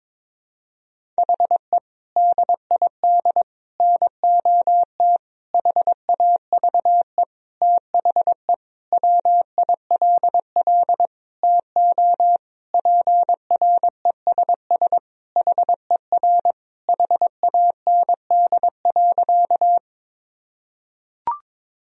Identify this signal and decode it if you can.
Morse code, heard as a keyed tone, about 22 wpm, 700 Hz. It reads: HE DID NOT HAVE THE WILL TO PRESS HER HAND.
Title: Sentences from Top 1000 Words - Review Part 1 - 22wpm